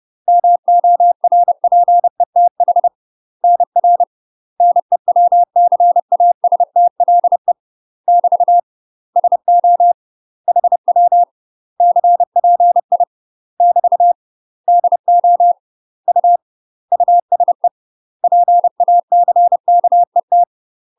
Morse code, heard as a keyed tone, about 30 wpm, 700 Hz. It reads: MORPETH NR NEWCASTLE <BT> SO HW CPI <BT> DO U USE PACKET
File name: UK MASTER QSOs TRAINING/30WPM